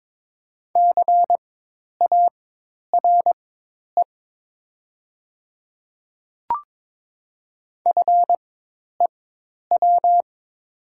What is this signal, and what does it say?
Morse code, heard as a keyed tone, about 22 wpm, spaced as for 10 wpm, 700 Hz. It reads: CARE FEW
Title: 4x Character Spacing - Top 300 Words - 22wpm